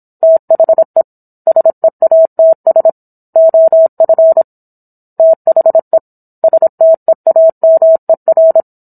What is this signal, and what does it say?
Morse code, heard as a keyed tone, about 27 wpm, 650 Hz. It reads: THE SEATS OF THE STEAMER